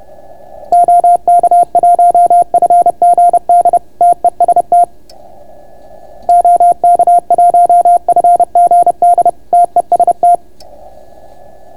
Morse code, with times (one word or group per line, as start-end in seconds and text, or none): 0.7-3.8 OK1FGD
4.0-4.8 TEST
6.3-9.3 OK1FGD
9.5-10.4 TEST